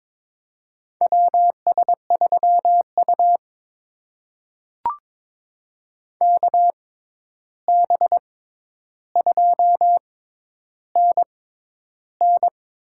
Morse code, heard as a keyed tone, about 22 wpm, spaced as for 11 wpm, 700 Hz. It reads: WS3U KB2NN
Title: USA Call Signs - 6x Character Spacing - 22wpm